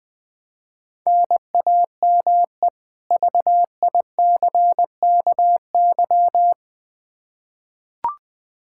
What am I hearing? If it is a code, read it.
Morse code, heard as a keyed tone, about 20 wpm, 700 Hz. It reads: NAME VICKY
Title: Practice QSO - Exchange 1 - Beginning - 20wpm